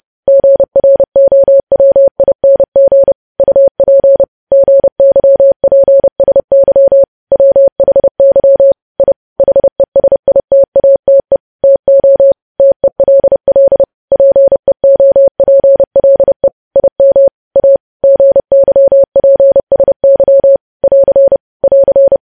Morse code, heard as a keyed tone, about 30 wpm, 550 Hz. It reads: GROWING UP GYPSY WHY I HESITATE TO TELL PEOPLE IM A GYPSY <AR> <AR>
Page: news headlines in morse code at 30 wpm